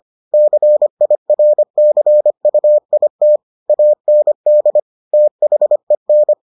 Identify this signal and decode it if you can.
Morse code, heard as a keyed tone, about 25 wpm, 600 Hz. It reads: CIRCUIT AND THEN